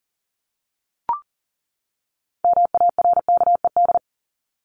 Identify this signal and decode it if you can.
Morse code, heard as a keyed tone, about 40 wpm, 700 Hz. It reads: MARKED
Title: ED Suffix - Top 100 Words with Base Words 40wpm